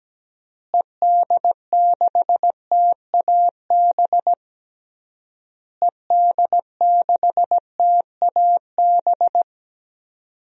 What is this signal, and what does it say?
Morse code, heard as a keyed tone, about 17 wpm, 700 Hz. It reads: ED6TAB ED6TAB
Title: DXCC-100 Call Signs - 6 Characters - Repeated 3x - 17wpm